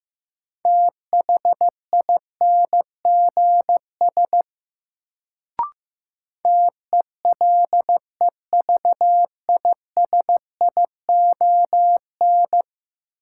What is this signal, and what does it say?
Morse code, heard as a keyed tone, about 15 wpm, 700 Hz. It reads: THINGS TELEVISION
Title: Sentences from Top 1000 Words - Review Part 4 - New Word Drill - 15wpm